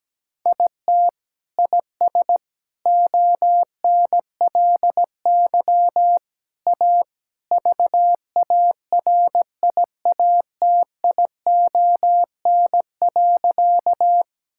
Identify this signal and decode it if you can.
Morse code, heard as a keyed tone, about 17 wpm, 700 Hz. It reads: IT IS ONLY A VARIATION.
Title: Sentences from Top 1600 Words 17wpm